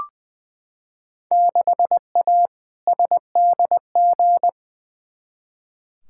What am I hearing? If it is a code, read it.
Morse code, heard as a keyed tone, about 21 wpm, 700 Hz. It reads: 6A SDG
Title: Field Day Exchanges Encore - 20wpm